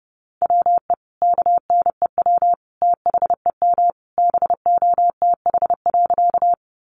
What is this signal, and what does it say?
Morse code, heard as a keyed tone, about 30 wpm, 700 Hz. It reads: WE KNEW THEM BOTH.
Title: Sentences from Top 400 Words 30wpm